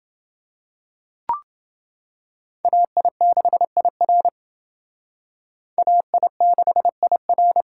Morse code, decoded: AI6IR AI6IR

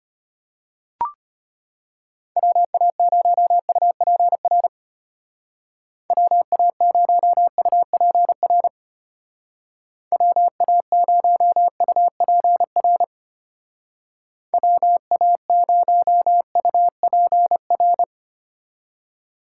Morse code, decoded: WA0UPR WA0UPR WA0UPR WA0UPR

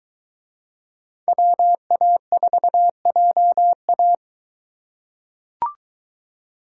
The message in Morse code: WA4JA

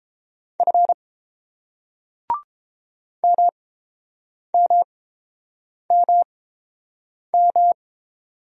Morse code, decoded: F MMMM

